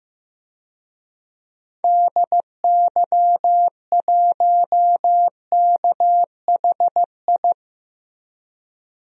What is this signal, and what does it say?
Morse code, heard as a keyed tone, about 15 wpm, 700 Hz. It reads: DY1KHI
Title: DXCC-100 Call Signs - 6 Characters - Repeated 2x - 15wpm